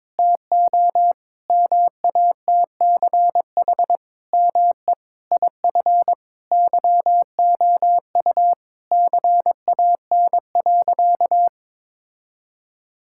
Morse code, heard as a keyed tone, about 22 wpm, 700 Hz. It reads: TO MATCH ME IF YOU CAN.
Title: Sentences from Top 1600 Words 22wpm